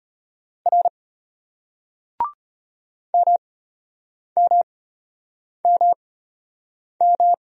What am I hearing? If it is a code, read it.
Morse code, heard as a keyed tone, about 33 wpm, spaced as for 8 wpm, 700 Hz. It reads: R MMMM